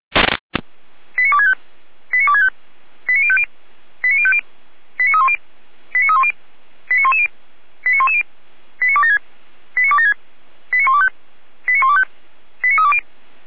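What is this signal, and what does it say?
Five-tone selective call (ZVEI), heard as quick runs of five tones, tones of about 70 ms. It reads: 89265 89265 89050 89050 89310 89310 89110 89110 89276 89276 89125 89125 89320
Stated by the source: ZVEI five-tone selective call received by SDR.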